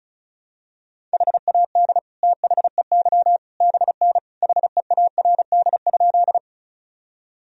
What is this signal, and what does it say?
Morse code, heard as a keyed tone, about 36 wpm, 700 Hz. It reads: HAD THEY BN HEARD?
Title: Sentences from Top 400 Words and Abbreviations 35wpm